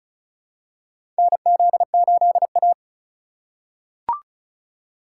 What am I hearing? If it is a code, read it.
Morse code, heard as a keyed tone, about 36 wpm, 700 Hz. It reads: NZ8A
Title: USA Call Signs - 4 Characters - 4x Character Spacing - 35wpm